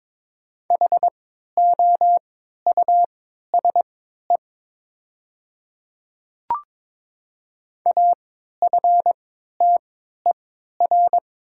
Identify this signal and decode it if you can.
Morse code, heard as a keyed tone, about 22 wpm, spaced as for 12 wpm, 700 Hz. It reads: HOUSE AFTER
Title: 3x Character Spacing - Top 200 Words - 22wpm